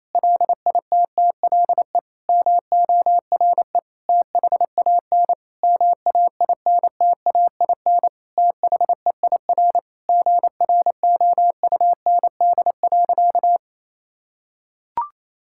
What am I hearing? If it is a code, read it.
Morse code, heard as a keyed tone, about 28 wpm, 700 Hz. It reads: LITTLE MORE THAN MAINTAIN THEIR GROUND.
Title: Sentences from Top 1000 Words - Review Part 5 - 28wpm